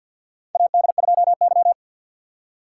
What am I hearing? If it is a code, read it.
Morse code, heard as a keyed tone, about 52 wpm, 700 Hz. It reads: AD2Y
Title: USA Call Signs - 2x Character Spacing - 50wpm